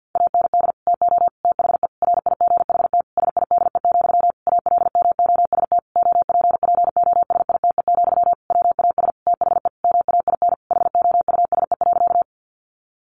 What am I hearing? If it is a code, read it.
Morse code, heard as a keyed tone, about 50 wpm, 700 Hz. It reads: AND TO THE RIGHT SIDE, ALMOST OPPOSITE, WAS THE MAIN HOUSE.
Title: Sentences from Top 1300 Words 50wpm